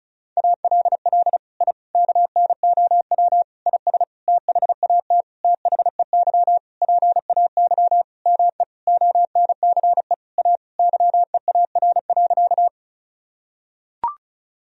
Morse code, heard as a keyed tone, about 34 wpm, 700 Hz. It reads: ALL I KNOW IS THAT THEY PAY ME ONCE A YEAR.
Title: Sentences from Top 1000 Words - Review Part 1 - 35wpm